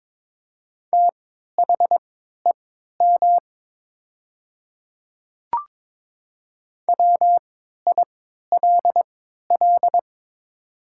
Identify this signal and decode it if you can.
Morse code, heard as a keyed tone, about 22 wpm, spaced as for 12 wpm, 700 Hz. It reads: THEM WILL
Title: 3x Character Spacing - Top 100 Words - 22wpm